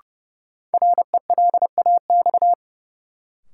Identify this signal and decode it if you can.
Morse code, heard as a keyed tone, about 30 wpm, 700 Hz. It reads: RELAX
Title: Sentences from Top 1400 Words - New Word Drill - 30wpm